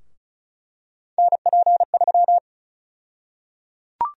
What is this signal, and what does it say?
Morse code, heard as a keyed tone, about 36 wpm, 700 Hz. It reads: NP3
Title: USA Call Sign Prefixes - 35wpm